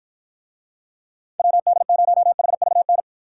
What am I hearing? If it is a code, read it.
Morse code, heard as a keyed tone, about 53 wpm, 700 Hz. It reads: WD0HVN